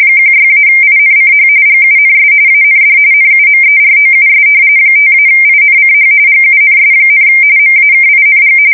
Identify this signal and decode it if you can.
RTTY, about 45.45 baud. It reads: 16-INCH GUNS AND NUMEROUS 5-INCH GUNS, MAKING
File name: fsk170 / battleshipnj